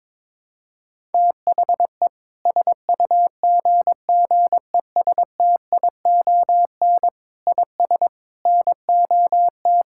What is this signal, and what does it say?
Morse code, heard as a keyed tone, about 22 wpm, 700 Hz. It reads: THE SUGGESTION IS NOT
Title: Sentences from Top 1600 Words 22wpm